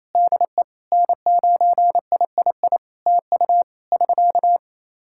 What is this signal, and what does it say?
Morse code, heard as a keyed tone, about 28 wpm, 700 Hz. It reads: DE N9III TU <SK>
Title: Practice QSO - Exchange 4 - Ending - 28wpm